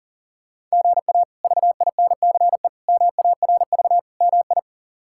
Morse code, heard as a keyed tone, about 40 wpm, 700 Hz. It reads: GA VINCE MARV MI
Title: K1USN SST QSO Practice - 40wpm